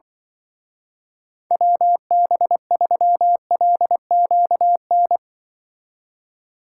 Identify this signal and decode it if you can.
Morse code, heard as a keyed tone, about 25 wpm, 700 Hz. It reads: WB3LQN
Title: Speed Racing - USA 2x3 Call Signs 20wpm